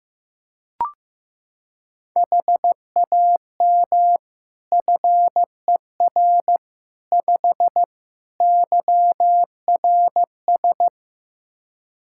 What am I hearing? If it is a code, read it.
Morse code, heard as a keyed tone, about 15 wpm, 700 Hz. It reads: HAM FER 5 YRS
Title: Practice QSO - Exchange 3 - More Info - 15wpm